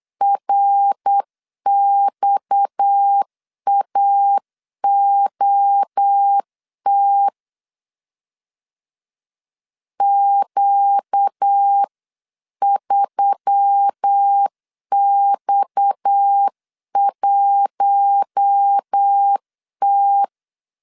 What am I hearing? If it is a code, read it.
Morse code, heard as a keyed tone, about 8 wpm, 800 Hz. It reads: RXAOT Q3X1T
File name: CW / DET / 1 6 1500C S C 8wpm 5219